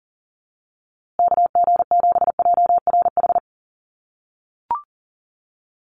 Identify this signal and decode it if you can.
Morse code, heard as a keyed tone, about 40 wpm, 700 Hz. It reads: KG7JRH
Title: USA Call Signs Repeated 2x 40wpm